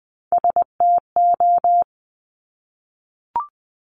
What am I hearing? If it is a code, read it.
Morse code, heard as a keyed tone, about 20 wpm, 700 Hz. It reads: STO
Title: Top Three Letter Combinations in Words 20wpm